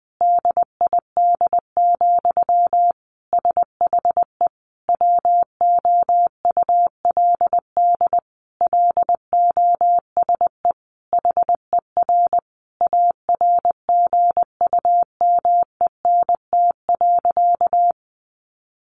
Morse code, heard as a keyed tone, about 20 wpm, 700 Hz. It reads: DID, SHE WOULD LOSE HER ARGUMENT.
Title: Sentences from Top 1400 Words 20wpm